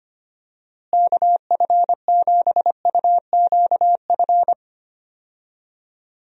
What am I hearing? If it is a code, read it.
Morse code, heard as a keyed tone, about 25 wpm, 700 Hz. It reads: KF7UQF
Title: USA Call Signs - 6 Characters - Repeated 2x - 25wpm